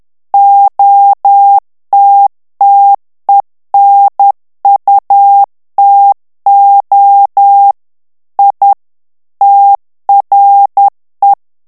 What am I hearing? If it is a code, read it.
Morse code, heard as a keyed tone, about 10 wpm, 800 Hz. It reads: OTTENUTO I TRE
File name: ITALIEN texte clair 2 vitesse 10 mots minute